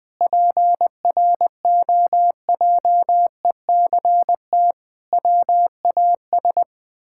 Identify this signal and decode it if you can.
Morse code, heard as a keyed tone, about 19 wpm, 700 Hz. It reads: PROJECT WAS
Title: Sentences with 9-Word Phrases - 20wpm